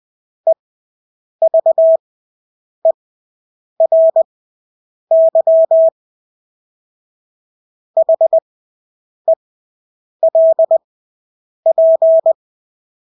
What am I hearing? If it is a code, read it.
Morse code, heard as a keyed tone, about 20 wpm, spaced as for 8 wpm, 650 Hz. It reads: EVERY HELP
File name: Engelske ord 20/8 wpm